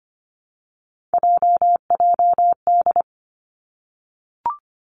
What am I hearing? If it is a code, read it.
Morse code, heard as a keyed tone, about 25 wpm, 700 Hz. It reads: JJD
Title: Three Challenging Letter-Number Combinations 25wpm